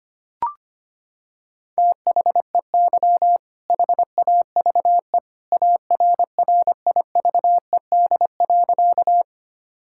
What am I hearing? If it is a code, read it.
Morse code, heard as a keyed tone, about 25 wpm, 700 Hz. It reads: THEY HAVE ARRIVED.
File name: practice / 25wpm / Sentences with 2-Word Phrases - 25wpm